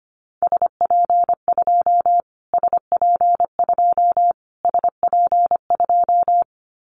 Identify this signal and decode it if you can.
Morse code, heard as a keyed tone, about 25 wpm, 700 Hz. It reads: SP2 SP2 SP2
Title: Callsign Prefixes for Top 25 DXCC - Repeated 3x - 25wpm